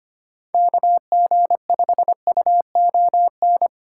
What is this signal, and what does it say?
Morse code, heard as a keyed tone, about 33 wpm, 700 Hz. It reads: KG5UON